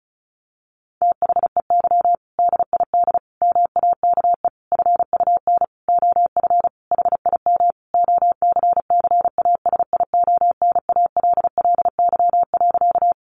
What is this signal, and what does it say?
Morse code, heard as a keyed tone, about 34 wpm, 700 Hz. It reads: THEY DID MAKE FUN OF HIM OCCASIONALLY.
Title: Sentences from Top 1300 Words 35wpm